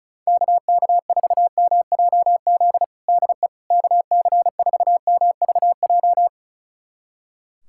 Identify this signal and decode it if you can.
Morse code, heard as a keyed tone, about 34 wpm, 700 Hz. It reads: KK4MJZ DE KC4MVJ
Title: USA Callsign DE Callsign - 6 Characters - 35wpm